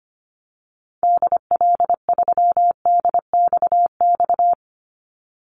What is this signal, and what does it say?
Morse code, heard as a keyed tone, about 25 wpm, 700 Hz. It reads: DL3DXX
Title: Call Signs - 25wpm